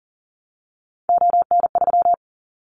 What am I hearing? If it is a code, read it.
Morse code, heard as a keyed tone, about 40 wpm, 700 Hz. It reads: ON3
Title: Callsign Prefixes for Top 25 DXCC - 40wpm